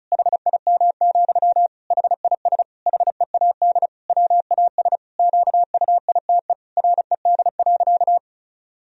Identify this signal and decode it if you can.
Morse code, heard as a keyed tone, about 36 wpm, 700 Hz. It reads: HIM, HIS HEAD WAS QUITE RED.
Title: Sentences from Top 1000 Words - Review Part 2 - 35wpm